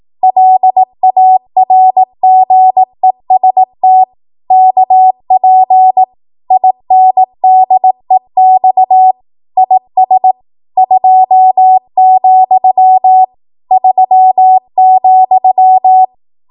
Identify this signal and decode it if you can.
Morse code, heard as a keyed tone, about 18 wpm, 750 Hz. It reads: LARGEST KP INDEX IS 2, 3,
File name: Morse / 18WPM